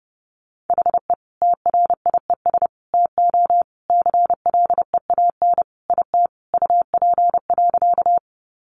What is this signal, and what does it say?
Morse code, heard as a keyed tone, about 30 wpm, 700 Hz. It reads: HE TRIES TO CLEAN IT UP.
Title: Additional Easy Phrases 30wpm